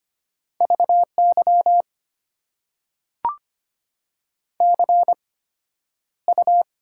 Morse code, heard as a keyed tone, about 25 wpm, spaced as for 11 wpm, 690 Hz. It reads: VY CU
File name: Top 90 QSO Elements - 8x Character Spacing - 25wpm